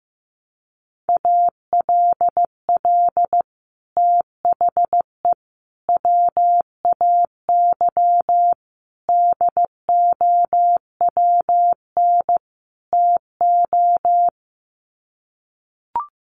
Morse code, heard as a keyed tone, about 15 wpm, 700 Hz. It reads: ALL THE WAY DOWN TO
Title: Sets of 5 Words Encore2 15wpm